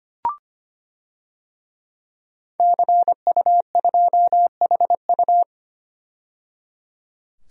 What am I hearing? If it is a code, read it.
Morse code, heard as a keyed tone, about 25 wpm, 700 Hz. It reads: CU2HU
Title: DXCC-100 Call Signs - 5 Characters - 25wpm